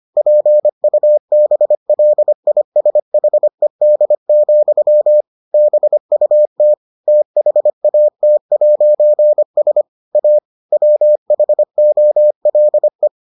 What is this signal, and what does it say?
Morse code, heard as a keyed tone, about 25 wpm, 600 Hz. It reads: PUBLISHED, BUT THAT'S A WHOLE